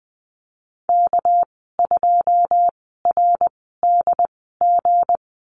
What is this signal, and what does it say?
Morse code, heard as a keyed tone, about 20 wpm, spaced as for 14 wpm, 700 Hz. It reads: K2RDG
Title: 213 Lesson 34g - Number 8 - Call Signs 20wpm